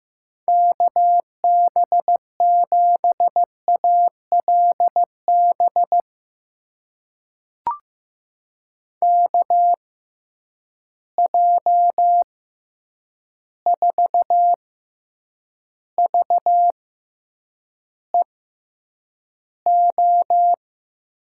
Morse code, heard as a keyed tone, about 15 wpm, spaced as for 8 wpm, 700 Hz. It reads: KB7ALB KJ4VEO
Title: USA Call Signs - 6 Characters - 6x Character Spacing - 15wpm